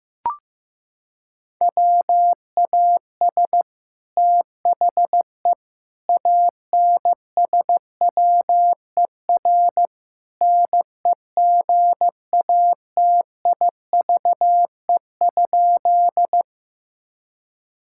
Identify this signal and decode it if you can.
Morse code, heard as a keyed tone, about 15 wpm, 700 Hz. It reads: WAS THE ANSWER NEGATIVE?
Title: Sentences from Top 1000 Words - Review Part 6 - 15wpm